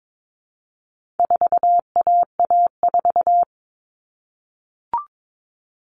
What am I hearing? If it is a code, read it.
Morse code, heard as a keyed tone, about 22 wpm, 700 Hz. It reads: 4AA4
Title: All Cut Numbers - 4-Digit 22wpm